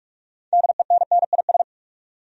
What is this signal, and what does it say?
Morse code, heard as a keyed tone, about 45 wpm, 700 Hz. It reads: DENNIS
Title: USA Common Names - 45wpm